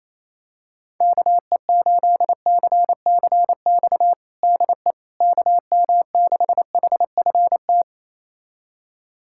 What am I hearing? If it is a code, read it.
Morse code, heard as a keyed tone, about 28 wpm, 700 Hz. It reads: KE8CCX DE KM6HFT
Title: USA Callsign DE Callsign - 6 Characters - 28wpm